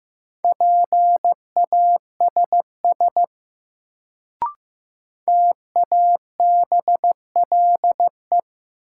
Morse code, heard as a keyed tone, about 15 wpm, 700 Hz. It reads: PASS TABLE
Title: Sentences from Top 400 Words - New Word Drill - 15wpm